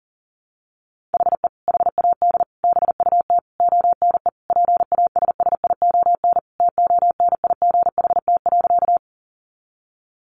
Morse code, heard as a keyed tone, about 40 wpm, 700 Hz. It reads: HE HAD BUT ONE PASSION TONIGHT.
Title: Sentences from Top 1300 Words 40wpm